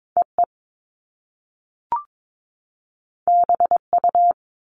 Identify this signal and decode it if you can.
Morse code, heard as a keyed tone, about 22 wpm, 700 Hz. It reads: EE BU
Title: Top Two Letter Combinations in Words 22wpm